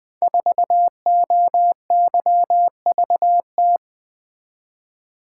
Morse code, heard as a keyed tone, about 20 wpm, 700 Hz. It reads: 4OYVT
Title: Code Groups - 5 Letter:Numbers - 20wpm